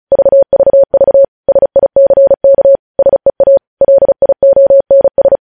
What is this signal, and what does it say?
Morse code, heard as a keyed tone, about 36 wpm, 550 Hz. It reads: VVV SICK SEA LIONS
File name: podcast / News headlines-1x-35x35